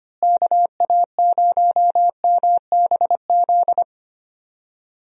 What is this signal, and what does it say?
Morse code, heard as a keyed tone, about 25 wpm, 700 Hz. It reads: KA0MBZ
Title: USA Call Signs - 6 Characters - Repeated 2x - 25wpm